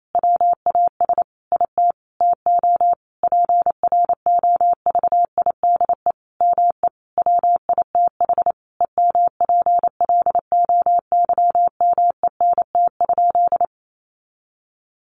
Morse code, heard as a keyed tone, about 28 wpm, 700 Hz. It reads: WAS IT TO PROVIDE ME WITH EMPLOYMENT?
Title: Sentences from Top 1500 Words 28wpm